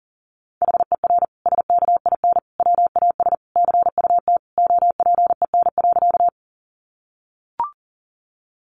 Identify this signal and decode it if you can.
Morse code, heard as a keyed tone, about 37 wpm, 700 Hz. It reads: HER SKIN WAS CUT OPEN.
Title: Sentences from Top 1300 Words 40wpm